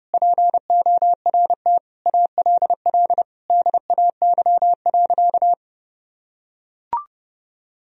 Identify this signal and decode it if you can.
Morse code, heard as a keyed tone, about 30 wpm, 700 Hz. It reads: PORT ALL DAY.
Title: Sentences from Top 1000 Words - Review Part 1 - 30wpm